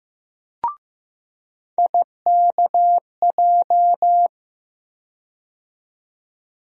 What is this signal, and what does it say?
Morse code, heard as a keyed tone, about 15 wpm, 700 Hz. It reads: IKJ
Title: USA Call Sign Postfixes - 3 Letters 15wpm